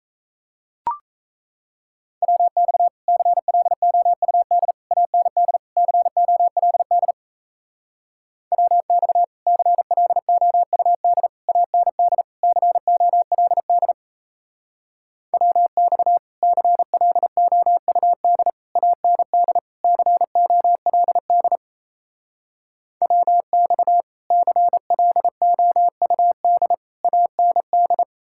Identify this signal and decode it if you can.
Morse code, heard as a keyed tone, about 36 wpm, 700 Hz. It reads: WX CLOUD AND COLD WX CLOUD AND COLD WX CLOUD AND COLD WX CLOUD AND